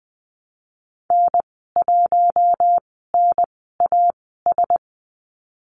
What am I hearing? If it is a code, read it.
Morse code, heard as a keyed tone, about 20 wpm, spaced as for 14 wpm, 700 Hz. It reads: N1NAS
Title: 082 Lesson 14g - Number 2 - Call Signs 20wpm